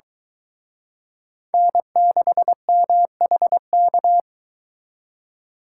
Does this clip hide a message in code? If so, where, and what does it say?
Yes: Morse code from 1.5 to 4.2 s, heard as a keyed tone, about 23 wpm, 700 Hz. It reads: N6MHK